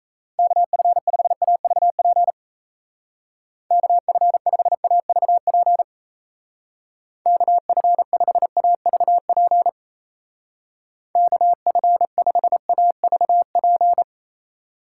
Morse code, heard as a keyed tone, about 36 wpm, 700 Hz. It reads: KF5AVP KF5AVP KF5AVP KF5AVP